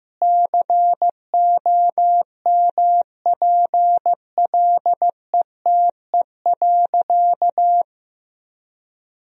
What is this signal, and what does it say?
Morse code, heard as a keyed tone, about 15 wpm, 700 Hz. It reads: COMPLETE.
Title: 2x Word-Spacing - Sentences from Top 400 Words - 15wpm